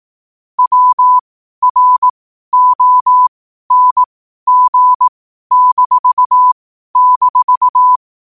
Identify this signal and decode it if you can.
Morse code, heard as a keyed tone, about 18 wpm, spaced as for 12 wpm, 1000 Hz. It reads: WRONG--
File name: qotdfiles / 12wpm